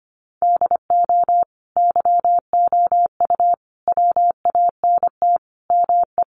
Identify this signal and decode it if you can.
Morse code, heard as a keyed tone, about 25 wpm, 700 Hz. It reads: DO YOU WANT ME
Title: Sets of 5 Words 25wpm